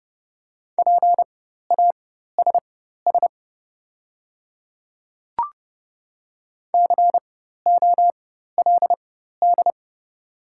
Morse code, heard as a keyed tone, about 30 wpm, spaced as for 14 wpm, 690 Hz. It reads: PASS COLD
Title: 4x Character Spacing - Top 500 Words - 30wpm